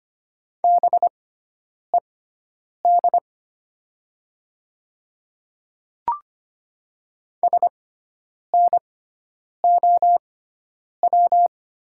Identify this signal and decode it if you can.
Morse code, heard as a keyed tone, about 25 wpm, spaced as for 9 wpm, 700 Hz. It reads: BED SNOW
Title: 6x Character Spacing - Top 500 Words - 25wpm